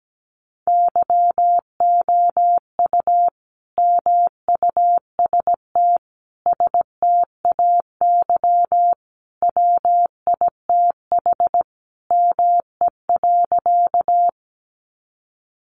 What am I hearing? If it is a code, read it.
Morse code, heard as a keyed tone, about 17 wpm, 700 Hz. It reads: YOU MUST STAY WITH ME.
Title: Sentences from Top 500 Words 17wpm